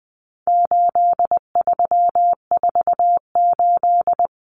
Morse code, heard as a keyed tone, about 20 wpm, 700 Hz. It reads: 8348
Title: Common Cut Numbers - 4-Digit 20wpm